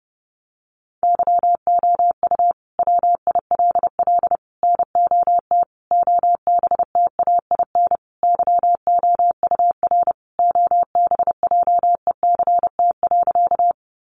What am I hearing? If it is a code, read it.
Morse code, heard as a keyed tone, about 30 wpm, 700 Hz. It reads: YOU WILL NOT OBTAIN YOUR OBJECT.
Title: Sentences from Top 1300 Words 30wpm